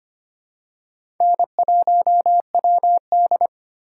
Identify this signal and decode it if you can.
Morse code, heard as a keyed tone, about 25 wpm, 700 Hz. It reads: N1WD
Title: USA Call Signs - 4 Characters - 8x Character Spacing - 25wpm